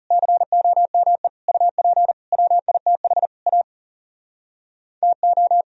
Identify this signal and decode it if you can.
Morse code, heard as a keyed tone, about 37 wpm, 700 Hz. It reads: COME UP WITH A TO
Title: Speed-Racing - Sets of 5 Words 30wpm